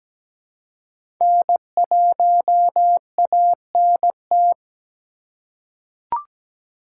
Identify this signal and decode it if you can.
Morse code, heard as a keyed tone, about 17 wpm, 700 Hz. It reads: N1ANT
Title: USA Call Signs - 6x Character Spacing - 17wpm